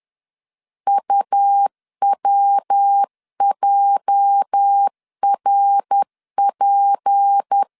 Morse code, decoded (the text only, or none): UWJRP